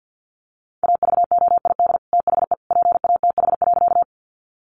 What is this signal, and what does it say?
Morse code, heard as a keyed tone, about 50 wpm, 700 Hz. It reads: AVOID THE PATH.
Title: Sentences from Top 1400 Words 50wpm